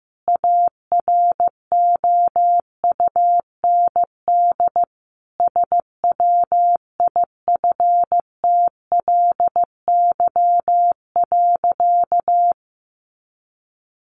Morse code, heard as a keyed tone, about 15 wpm, 700 Hz. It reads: AROUND SWIFTLY.
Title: A Sound Of Thunder 15wpm